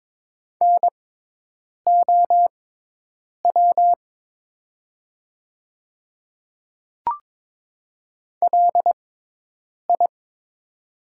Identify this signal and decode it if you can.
Morse code, heard as a keyed tone, about 22 wpm, spaced as for 8 wpm, 700 Hz. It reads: NOW LI
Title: 6x Character Spacing - Top 200 Words - 22wpm